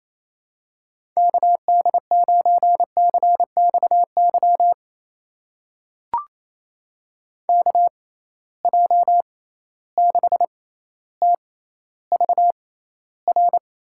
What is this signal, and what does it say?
Morse code, heard as a keyed tone, about 28 wpm, spaced as for 14 wpm, 700 Hz. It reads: KD9CXY KJ6TVR
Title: USA Call Signs - 6 Characters - 6x Character Spacing - 28wpm